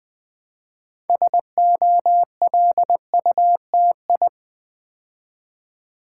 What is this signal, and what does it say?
Morse code, heard as a keyed tone, about 20 wpm, 700 Hz. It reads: SOLUTI
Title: Build A Word - 3 Letters At A Time - 8-Letter Words - 20wpm